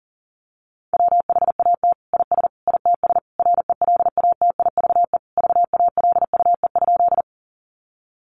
Morse code, heard as a keyed tone, about 40 wpm, 700 Hz. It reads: WHAT IS ITS RELATIVE VALUE?
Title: Sentences from Top 1400 Words 40wpm